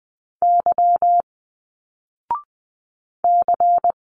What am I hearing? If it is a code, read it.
Morse code, heard as a keyed tone, about 20 wpm, 700 Hz. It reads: Y C